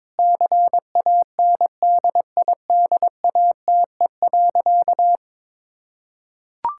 Morse code, heard as a keyed tone, about 21 wpm, 700 Hz. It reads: CANDIDATE.
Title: Sentences from Top 1600 Words 22wpm